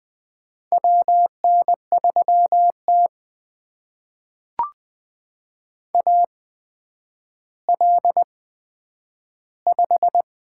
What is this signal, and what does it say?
Morse code, heard as a keyed tone, about 20 wpm, spaced as for 10 wpm, 700 Hz. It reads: WN3T AL5
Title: USA Call Signs - 4 Characters - 8x Character Spacing - 20wpm